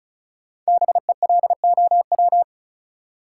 Morse code, heard as a keyed tone, about 36 wpm, 700 Hz. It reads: BELOW
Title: Sentences from Top 1000 Words - Review Part 5 - New Word Drill - 35wpm